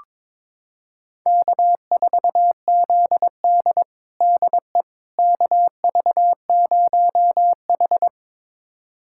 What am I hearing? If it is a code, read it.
Morse code, heard as a keyed tone, about 22 wpm, 700 Hz. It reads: K4ZD DE KV0H
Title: USA Callsign DE Callsign - 4 Characters - 22wpm